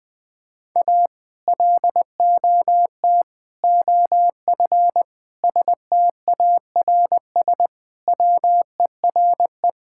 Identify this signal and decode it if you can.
Morse code, heard as a keyed tone, about 20 wpm, 700 Hz. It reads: A LOT OF STARS WERE
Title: Sentences with 2-Word Phrases - 20wpm